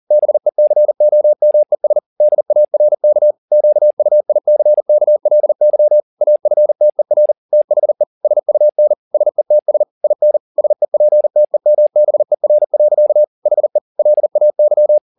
Morse code, heard as a keyed tone, about 40 wpm, 600 Hz. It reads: BECOMES DARK QUICKLY AFTER THE SUN SETS IN SEPTEMBER. HE LAY
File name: the old man and the sea 2 40wpm